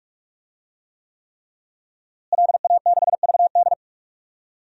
Morse code, heard as a keyed tone, about 46 wpm, 700 Hz. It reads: LA6VD
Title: DXCC-100 Call Signs - 5 Characters - Repeated 3x - 45wpm